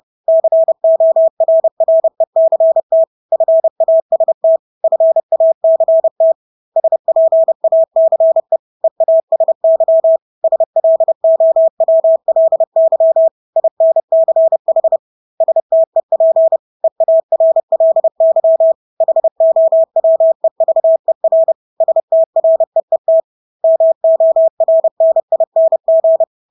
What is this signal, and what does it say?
Morse code, heard as a keyed tone, about 30 wpm, 650 Hz. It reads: CORRECT FAST FACT SPACE EASY SLOWLY INCH STEP EARLY HOWEVER STREET MORNING
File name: Engelske ord 30-30 wpm